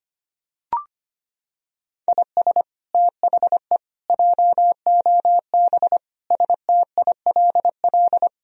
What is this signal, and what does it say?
Morse code, heard as a keyed tone, about 25 wpm, 700 Hz. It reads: IS THE JOB STILL
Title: Sentences from Top 1000 Words - Review Part 3 - 25wpm